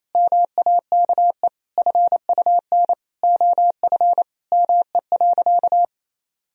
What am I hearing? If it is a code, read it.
Morse code, heard as a keyed tone, about 28 wpm, 700 Hz. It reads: MAKE FUN OF ME.
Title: Sentences from Top 1000 Words - Review Part 4 - 28wpm